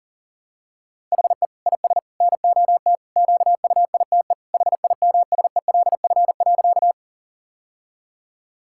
Morse code, heard as a keyed tone, about 40 wpm, 700 Hz. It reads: HE IS NOT QUITE HIMSELF.
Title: Sentences from Top 1600 Words 40wpm